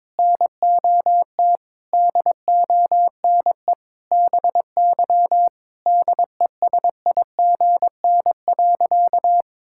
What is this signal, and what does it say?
Morse code, heard as a keyed tone, about 22 wpm, 700 Hz. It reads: NOT DONE BY DESIGN.
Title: Sentences from Top 1000 Words - Review Part 5 - 22wpm